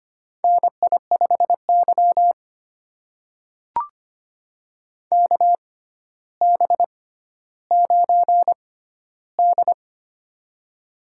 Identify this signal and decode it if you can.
Morse code, heard as a keyed tone, about 25 wpm, spaced as for 13 wpm, 700 Hz. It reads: NI5Y KB9D
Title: USA Call Signs - 4 Characters - 6x Character Spacing - 25wpm